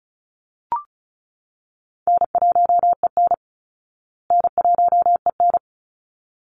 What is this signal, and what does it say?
Morse code, heard as a keyed tone, about 34 wpm, 700 Hz. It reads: N1EN N1EN
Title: Easy Call Signs Repeated 2x - 35wpm